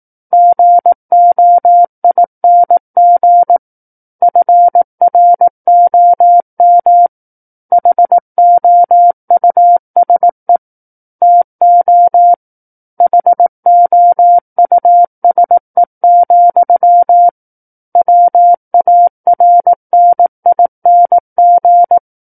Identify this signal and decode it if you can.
Morse code, heard as a keyed tone, about 18 wpm, 700 Hz. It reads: GOING FROM HOUSE TO HOUSE, WARNING
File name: Book One / 09-Chapter 9 - 18 WPM